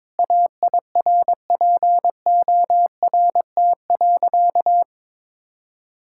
Morse code, heard as a keyed tone, about 22 wpm, 700 Hz. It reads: AIRPORT.
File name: practice / 22wpm / Sentences with 3-Word Phrases - 22wpm